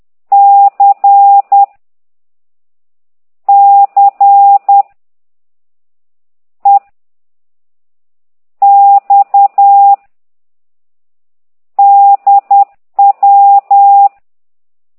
Morse code, heard as a keyed tone, about 10 wpm, 800 Hz. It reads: C C E X D W
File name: ct2idw / exerc13